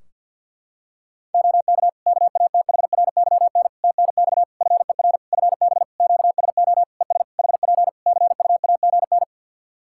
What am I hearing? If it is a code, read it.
Morse code, heard as a keyed tone, about 49 wpm, 700 Hz. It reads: OK KATHRYN TNX FER FB QSO ES HP CUAGN
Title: Practice QSO - Exchange 4 - Ending - 50wpm